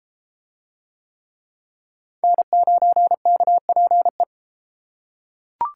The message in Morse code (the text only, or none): N9KPE